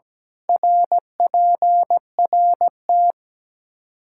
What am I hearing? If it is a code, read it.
Morse code, heard as a keyed tone, about 17 wpm, 700 Hz. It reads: RPRT
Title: Practice QSO - Exchange 2 - Info - 17wpm